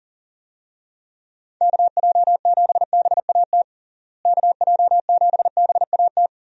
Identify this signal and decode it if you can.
Morse code, heard as a keyed tone, about 40 wpm, 700 Hz. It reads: KJ7BAT KJ7BAT
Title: USA Call Signs - 6 Characters - Repeated 3x - 40wpm